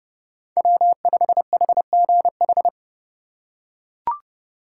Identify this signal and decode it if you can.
Morse code, heard as a keyed tone, about 29 wpm, 700 Hz. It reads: W5HGH